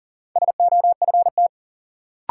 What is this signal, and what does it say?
Morse code, heard as a keyed tone, about 40 wpm, 700 Hz. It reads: SOFT
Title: Sentences from Top 1500 Words - New Word Drill - 40wpm